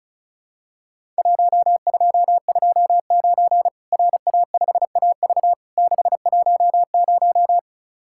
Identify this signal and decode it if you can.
Morse code, heard as a keyed tone, about 36 wpm, 700 Hz. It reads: 1229 RA5AV 610
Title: Worked All Europe DX Contest - QTC Exchange - 35wpm